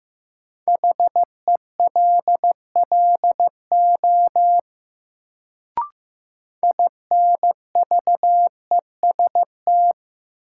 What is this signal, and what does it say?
Morse code, heard as a keyed tone, about 15 wpm, 700 Hz. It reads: HELLO INVEST
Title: Sentences from Top 1100 Words - New Word Drill - 15wpm